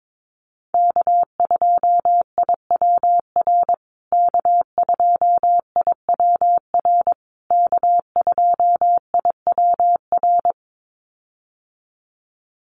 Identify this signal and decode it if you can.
Morse code, heard as a keyed tone, about 22 wpm, 700 Hz. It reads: K2IWR K2IWR K2IWR
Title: USA Call Signs Repeated 3x 22wpm